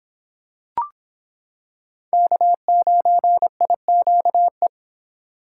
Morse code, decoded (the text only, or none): K9IQE